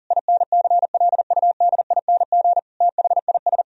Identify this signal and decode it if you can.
Morse code, heard as a keyed tone, about 40 wpm, 700 Hz. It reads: INCLUDING THIS
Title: Sentences from Top 1000 Words - Review Part 4 - 40wpm